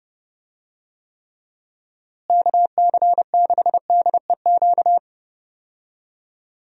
Morse code, heard as a keyed tone, about 30 wpm, 700 Hz. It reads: KC6DEQ
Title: USA Call Signs - 6 Characters - Repeated 2x - 30wpm